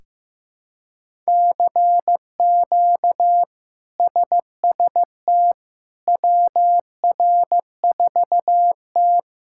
Morse code, heard as a keyed tone, about 15 wpm, 700 Hz. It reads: CQ SST WR4T
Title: K1USN SST QSO Practice - 15wpm